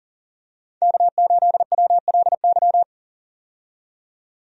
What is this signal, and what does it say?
Morse code, heard as a keyed tone, about 40 wpm, 700 Hz. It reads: K8WLY
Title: USA Call Signs - 5 Characters - 40wpm